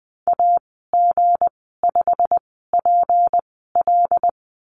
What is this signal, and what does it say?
Morse code, heard as a keyed tone, about 20 wpm, spaced as for 14 wpm, 700 Hz. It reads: AG5PL
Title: 173 Lesson 28g - Number 9 - Call Signs 20wpm